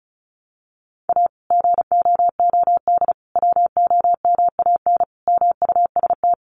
Morse code, heard as a keyed tone, about 34 wpm, 700 Hz. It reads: A GOOD WOMAN MUST
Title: Sentences from Top 1100 Words 35wpm